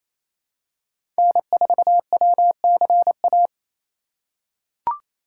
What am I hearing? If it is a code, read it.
Morse code, heard as a keyed tone, about 28 wpm, 700 Hz. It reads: N4WCA
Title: USA Call Signs - 5 Characters - 8x Character Spacing - 28wpm